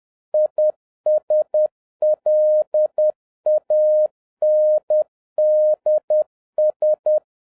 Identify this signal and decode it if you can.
Morse code, heard as a keyed tone, about 10 wpm, 610 Hz. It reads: ISLANDS